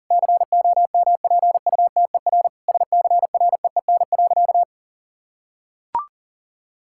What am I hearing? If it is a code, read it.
Morse code, heard as a keyed tone, about 37 wpm, 700 Hz. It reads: COMPUTER SCREEN.
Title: Sentences from Top 1000 Words - Review Part 6 - 40wpm